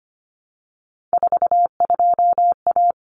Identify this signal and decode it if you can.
Morse code, heard as a keyed tone, about 25 wpm, 700 Hz. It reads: 42A
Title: Common Cut Numbers - 3-Digit 25wpm